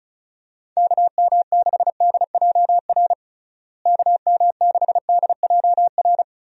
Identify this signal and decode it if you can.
Morse code, heard as a keyed tone, about 36 wpm, 700 Hz. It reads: KM6DJR KM6DJR
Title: USA Call Signs - 6 Characters - Repeated 3x - 35wpm